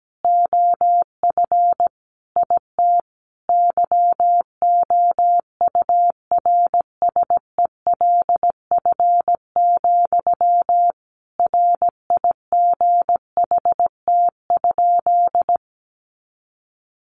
Morse code, heard as a keyed tone, about 17 wpm, 700 Hz. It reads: OF IT YOURSELF, RIGHT?
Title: Sentences from Top 1200 Words 17wpm